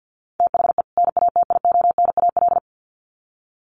Morse code, heard as a keyed tone, about 50 wpm, 700 Hz. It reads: THE NATIONAL
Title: Sets of 2 Words 50wpm